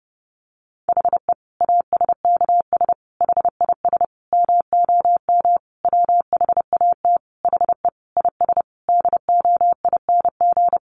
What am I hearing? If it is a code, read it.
Morse code, heard as a keyed tone, about 30 wpm, 700 Hz. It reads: HE ASKS HIS MOM WHAT HE IS DOING
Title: Easy Phrases 30wpm